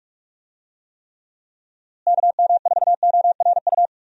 Morse code, heard as a keyed tone, about 43 wpm, 700 Hz. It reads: KM4ORU
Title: USA Call Signs - 6 Characters - Repeated 2x - 45wpm